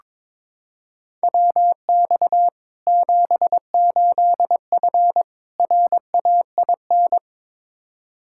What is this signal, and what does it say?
Morse code, heard as a keyed tone, about 22 wpm, 700 Hz. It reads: WX 78F RAIN
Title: K1USN SST QSO Practice 22wpm